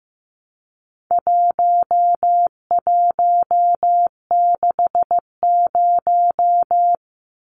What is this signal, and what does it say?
Morse code, heard as a keyed tone, about 15 wpm, 700 Hz. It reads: 1160
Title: Numbers - 4-Digit 15wpm